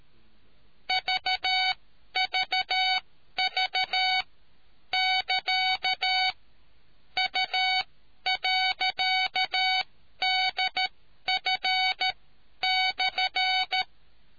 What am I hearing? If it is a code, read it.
Morse code, heard as a keyed tone, about 13 wpm, spaced as for 11 wpm, 750 Hz. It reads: VVV <KA> U.DF/